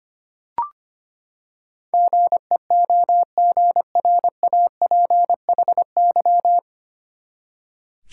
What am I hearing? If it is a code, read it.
Morse code, heard as a keyed tone, about 25 wpm, 700 Hz. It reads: GEOGRAPHY
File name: Nine Letter Words - 25wpm